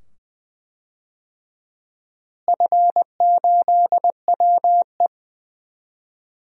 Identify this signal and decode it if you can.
Morse code, heard as a keyed tone, about 21 wpm, 700 Hz. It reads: F8WE
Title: DXCC-100 Call Signs - 4 Characters - Repeated 2x - 20wpm